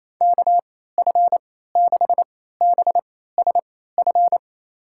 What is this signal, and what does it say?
Morse code, heard as a keyed tone, about 28 wpm, spaced as for 16 wpm, 700 Hz. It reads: KF6BSF
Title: USA Call Signs - 3x Character Spacing - 28wpm